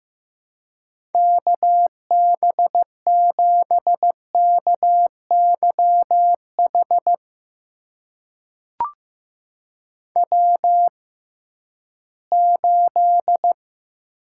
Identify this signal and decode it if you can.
Morse code, heard as a keyed tone, about 15 wpm, spaced as for 11 wpm, 700 Hz. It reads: KB7KYH W8
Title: USA Call Signs - 6x Character Spacing - 15wpm